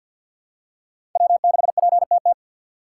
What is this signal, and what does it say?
Morse code, heard as a keyed tone, about 48 wpm, 700 Hz. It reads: W6PTT
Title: USA Call Signs - 5 Characters - 6x Character Spacing - 50wpm